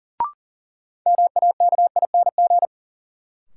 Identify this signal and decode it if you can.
Morse code, heard as a keyed tone, about 40 wpm, 700 Hz. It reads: MAKING
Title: Sentences from Top 1000 Words - Review Part 6 - New Word Drill - 40wpm